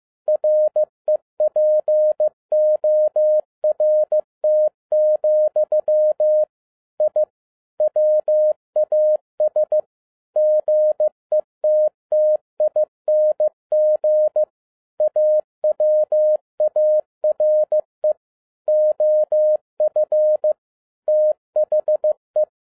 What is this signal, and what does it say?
Morse code, heard as a keyed tone, about 15 wpm, 610 Hz. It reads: REPORT, I WAS GETTING AWARE OF THE